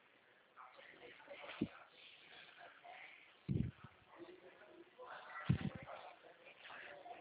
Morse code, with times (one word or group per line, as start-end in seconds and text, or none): none